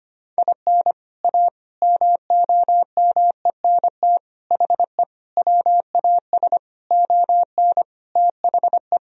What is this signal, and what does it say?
Morse code, heard as a keyed tone, about 25 wpm, 700 Hz. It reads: IN A MOMENT HE WAS ON THE
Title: Sentences from Top 1000 Words - Review Part 2 - 25wpm